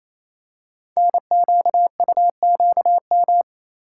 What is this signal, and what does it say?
Morse code, heard as a keyed tone, about 27 wpm, 700 Hz. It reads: NQUQM
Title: Code Groups - 5 Letters - 28wpm